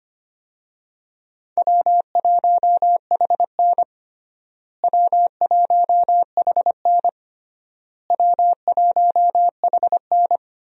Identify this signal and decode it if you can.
Morse code, heard as a keyed tone, about 25 wpm, 700 Hz. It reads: W1HN W1HN W1HN
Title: USA Call Signs - 4 Characters - Repeated 3x - 25wpm